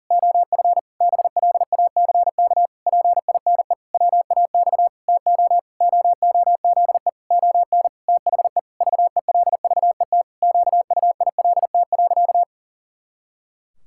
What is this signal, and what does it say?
Morse code, heard as a keyed tone, about 40 wpm, 700 Hz. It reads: OF BLACK PINE WAX TO OOZE ON THE VELVET QUILT.
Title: Pangram Challenge v2 - 40wpm